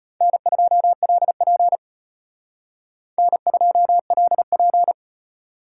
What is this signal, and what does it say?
Morse code, heard as a keyed tone, about 34 wpm, 700 Hz. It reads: N2LP N2LP